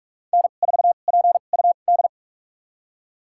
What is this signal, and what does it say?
Morse code, heard as a keyed tone, about 44 wpm, spaced as for 32 wpm, 700 Hz. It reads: N4PUD
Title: USA Call Signs - 5 Characters - 2x Character Spacing - 45wpm